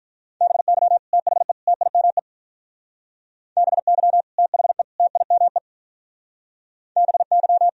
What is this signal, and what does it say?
Morse code, heard as a keyed tone, about 43 wpm, 700 Hz. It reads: BY THE TIME BY THE TIME BY
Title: Speed-Racing - Sets of 3 Words 35wpm